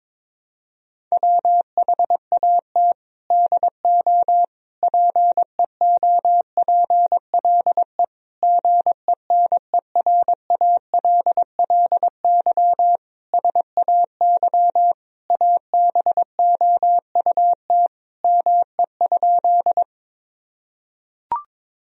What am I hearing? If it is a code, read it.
Morse code, heard as a keyed tone, about 21 wpm, 700 Hz. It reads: WHAT DO PEOPLE GENERALLY SAY ABOUT ME?
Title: Sentences from Top 1000 Words - Review Part 4 - 22wpm